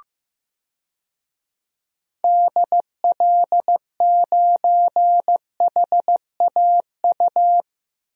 Morse code, heard as a keyed tone, about 16 wpm, 700 Hz. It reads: DL9HAU
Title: DXCC-100 Call Signs - 6 Characters - Repeated 3x - 15wpm